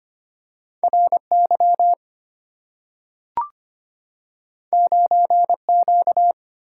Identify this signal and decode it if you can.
Morse code, heard as a keyed tone, about 25 wpm, 700 Hz. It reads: RY 9Q
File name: Two Letter-Number Combinations - 25wpm